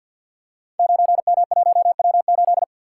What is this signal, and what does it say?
Morse code, heard as a keyed tone, about 50 wpm, 700 Hz. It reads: 9M1W8
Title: Code Groups - 5 Letter:Numbers - 50wpm